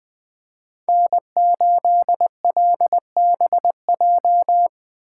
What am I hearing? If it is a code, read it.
Morse code, heard as a keyed tone, about 20 wpm, 700 Hz. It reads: N8LBJ
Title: Speed Racing - USA 1x3 Call Signs 17wpm